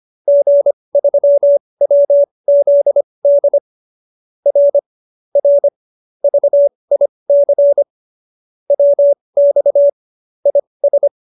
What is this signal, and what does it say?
Morse code, heard as a keyed tone, about 25 wpm, spaced as for 20 wpm, 570 Hz. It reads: G3WZD R R VIC WX IS
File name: QSO310 20